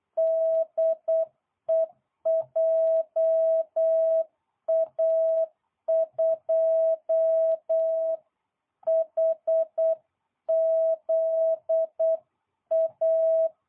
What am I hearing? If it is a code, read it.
Morse code, heard as a keyed tone, about 8 wpm, 660 Hz. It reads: DE JA2HZA